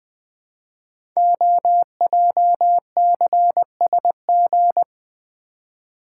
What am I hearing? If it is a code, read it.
Morse code, heard as a keyed tone, about 20 wpm, 700 Hz. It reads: OJCSG
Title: Code Groups - 5 Letters - 20wpm